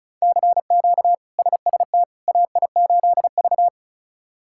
Morse code, heard as a keyed tone, about 34 wpm, 700 Hz. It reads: CQ SST AI8V
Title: K1USN SST QSO Practice - 35wpm